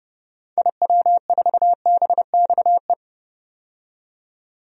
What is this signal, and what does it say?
Morse code, heard as a keyed tone, about 32 wpm, 700 Hz. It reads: IW4BXE
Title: DXCC-100 Call Signs - 6 Characters - 30wpm